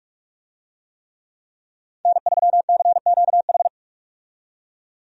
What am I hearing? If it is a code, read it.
Morse code, heard as a keyed tone, about 46 wpm, 700 Hz. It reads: N3CQH
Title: USA Call Signs - 5 Characters - 45wpm